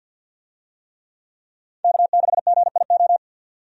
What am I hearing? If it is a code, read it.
Morse code, heard as a keyed tone, about 50 wpm, 700 Hz. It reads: K6GIO
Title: USA Call Signs - 5 Characters - 50wpm